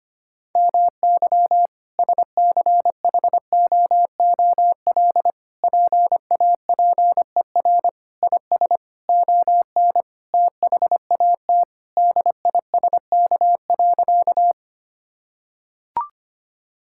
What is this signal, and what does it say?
Morse code, heard as a keyed tone, about 25 wpm, 700 Hz. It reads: MY SCHOOL PAPER IS ON THAT DISK.
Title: Sentences from Top 1000 Words - Review Part 6 - 25wpm